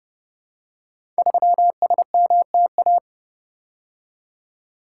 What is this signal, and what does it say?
Morse code, heard as a keyed tone, about 28 wpm, 700 Hz. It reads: 3SMTA
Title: Code Groups - 5 Letter:Numbers - 30wpm